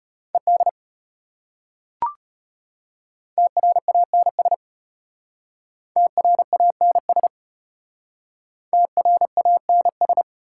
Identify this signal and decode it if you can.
Morse code, heard as a keyed tone, about 33 wpm, 690 Hz. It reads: ED TRANS TRANS TRANS